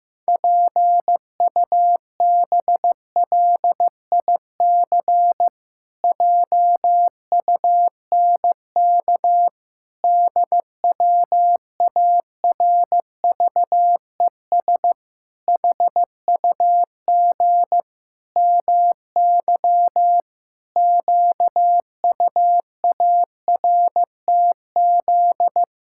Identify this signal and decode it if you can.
Morse code, heard as a keyed tone, about 16 wpm, 700 Hz. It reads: PUBLIC JUNK DWARVES HUG MY QUARTZ
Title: Pangram Challenge v2 - 15wpm